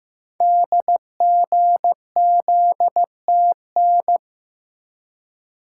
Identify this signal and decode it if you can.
Morse code, heard as a keyed tone, about 15 wpm, 700 Hz. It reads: DGZTN
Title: Code Groups - 5 Letter:Numbers - 15wpm